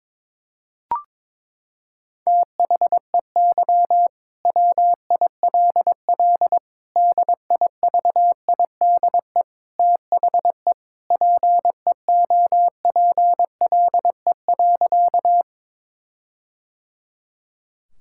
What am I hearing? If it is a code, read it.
Morse code, heard as a keyed tone, about 22 wpm, 700 Hz. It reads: THEY WILL DIVIDE THE PEOPLE.
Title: Sentences from Top 1600 Words 22wpm